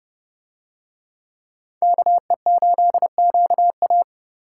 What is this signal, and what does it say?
Morse code, heard as a keyed tone, about 30 wpm, 700 Hz. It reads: KE8QA